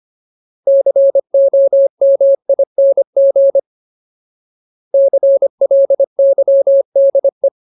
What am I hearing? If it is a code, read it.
Morse code, heard as a keyed tone, about 25 wpm, 550 Hz. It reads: COMING CLYDE
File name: Ultimate Word List / 25WPM Ultimate